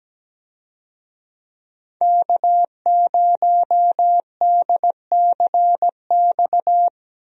Morse code, heard as a keyed tone, about 17 wpm, 700 Hz. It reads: K0DCX
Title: USA Call Signs - 5 Characters - Repeated 2x - 17wpm